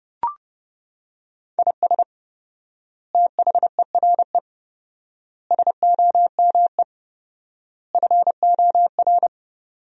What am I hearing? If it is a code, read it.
Morse code, heard as a keyed tone, about 32 wpm, 700 Hz. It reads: IS THERE SOME FOR
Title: 4x Word-Spacing - Sentences from Top 200 Words - 30wpm